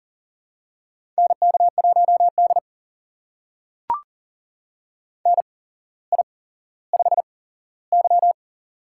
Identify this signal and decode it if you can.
Morse code, heard as a keyed tone, about 39 wpm, spaced as for 17 wpm, 700 Hz. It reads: NK1D NI5Y
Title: USA Call Signs - 4 Characters - 8x Character Spacing - 40wpm